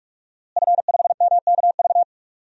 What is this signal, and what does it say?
Morse code, heard as a keyed tone, about 46 wpm, 700 Hz. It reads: F5MKV
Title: DXCC-100 Call Signs - 5 Characters - Repeated 3x - 45wpm